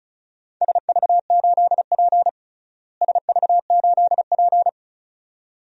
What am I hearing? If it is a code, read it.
Morse code, heard as a keyed tone, about 34 wpm, 700 Hz. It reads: SV8P SV8P
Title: DXCC-100 Call Signs - 4 Characters - Repeated 2x - 35wpm